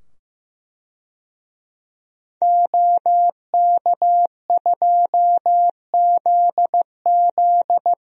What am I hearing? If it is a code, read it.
Morse code, heard as a keyed tone, about 15 wpm, 700 Hz. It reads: OK2ZZ
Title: DXCC-100 Call Signs - 5 Characters - 15wpm